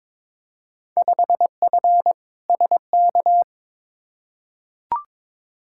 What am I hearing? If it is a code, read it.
Morse code, heard as a keyed tone, about 22 wpm, 700 Hz. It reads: 5F SK
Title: Field Day Exchanges Encore - 22wpm